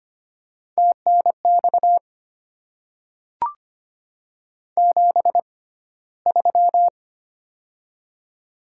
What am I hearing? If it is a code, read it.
Morse code, heard as a keyed tone, about 25 wpm, spaced as for 15 wpm, 700 Hz. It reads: TNX 73
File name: Top 20 QSO Elements - 6x Character Spacing - 25wpm